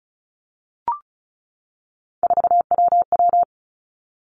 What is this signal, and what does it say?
Morse code, heard as a keyed tone, about 36 wpm, 700 Hz. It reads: 4WW
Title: Three Challenging Letter-Number Combinations 35wpm